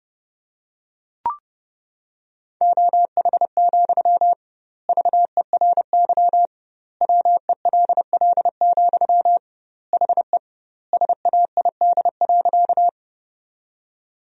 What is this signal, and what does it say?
Morse code, heard as a keyed tone, about 30 wpm, 700 Hz. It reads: OH, VERY WELL, HE SAID.
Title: 2x Word-Spacing - Sentences from Top 500 Words - 30wpm